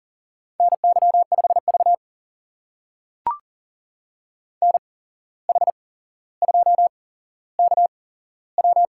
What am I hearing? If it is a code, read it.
Morse code, heard as a keyed tone, about 40 wpm, spaced as for 16 wpm, 700 Hz. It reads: NY5V NH2KW
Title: USA Call Signs - 8x Character Spacing - 40wpm